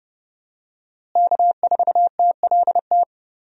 Morse code, heard as a keyed tone, about 30 wpm, 700 Hz. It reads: K4TLT
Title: USA Call Signs - 5 Characters - Repeated 3x - 30wpm